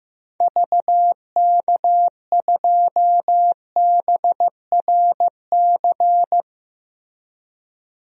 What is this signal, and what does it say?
Morse code, heard as a keyed tone, about 15 wpm, 700 Hz. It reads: VK2BRC
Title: DXCC-100 Call Signs - 6 Characters - Repeated 3x - 15wpm